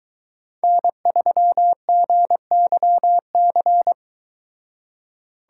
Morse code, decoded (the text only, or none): N3GYC